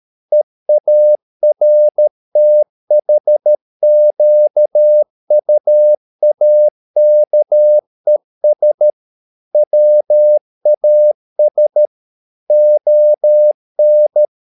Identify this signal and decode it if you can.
Morse code, heard as a keyed tone, about 13 wpm, 600 Hz. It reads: EARTHQUAKES WAS ON
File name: mp3 files / Lexiphiles - 13wpm